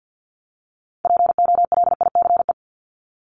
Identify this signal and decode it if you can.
Morse code, heard as a keyed tone, about 50 wpm, 700 Hz. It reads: POLICE
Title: Top 1751-2000 Words 50wpm